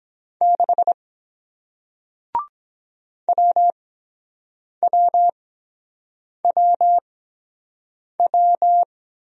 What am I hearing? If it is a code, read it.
Morse code, heard as a keyed tone, about 22 wpm, spaced as for 7 wpm, 700 Hz. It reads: 6 WWWW